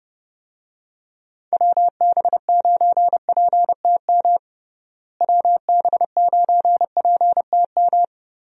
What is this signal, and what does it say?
Morse code, heard as a keyed tone, about 30 wpm, 700 Hz. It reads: WB9PTM WB9PTM
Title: USA Call Signs - 6 Characters - Repeated 3x - 30wpm